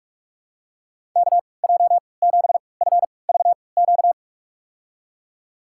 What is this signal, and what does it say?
Morse code, heard as a keyed tone, about 46 wpm, spaced as for 26 wpm, 700 Hz. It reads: KJ7FVQ
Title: USA Call Signs - 6 Characters - 3x Character Spacing - 45wpm